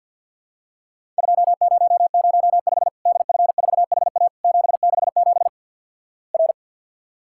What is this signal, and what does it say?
Morse code, heard as a keyed tone, about 50 wpm, 700 Hz. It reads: 2005 DF4HA 767 R
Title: Worked All Europe DX Contest - QTC Exchange - 50wpm